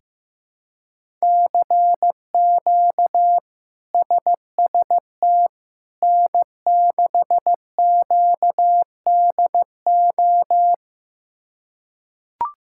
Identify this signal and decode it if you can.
Morse code, heard as a keyed tone, about 16 wpm, 700 Hz. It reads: CQ SST N6QDO
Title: K1USN SST QSO Practice - 15wpm